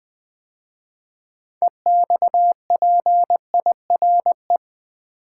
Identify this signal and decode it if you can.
Morse code, heard as a keyed tone, about 20 wpm, 700 Hz. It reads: EXPIRE
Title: 6 Letter Words - First 3 Letters Given - 20wpm